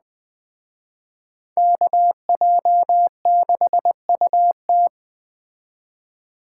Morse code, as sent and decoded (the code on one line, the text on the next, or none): -.- .--- -.... ..- -
KJ6UT